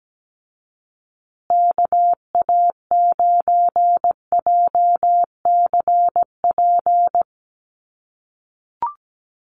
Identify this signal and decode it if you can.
Morse code, heard as a keyed tone, about 17 wpm, 700 Hz. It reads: KA9JCP
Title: USA Call Signs Repeated 2x 17wpm